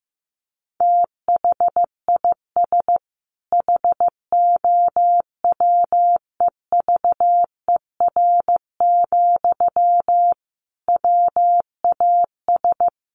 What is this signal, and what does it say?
Morse code, heard as a keyed tone, about 15 wpm, 700 Hz. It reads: THIS HOWEVER, WAS
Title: Sentences from Top 1400 Words 15wpm